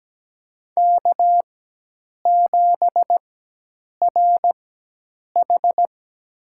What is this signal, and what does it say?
Morse code, heard as a keyed tone, about 17 wpm, spaced as for 8 wpm, 700 Hz. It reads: K7RH
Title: USA Call Signs - 4 Characters - 4x Character Spacing - 17wpm